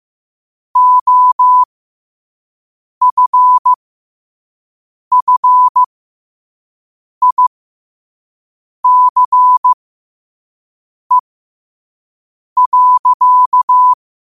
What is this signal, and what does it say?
Morse code, heard as a keyed tone, about 15 wpm, spaced as for 5 wpm, 1000 Hz. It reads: OFFICE.
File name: qotdfiles / 5wpm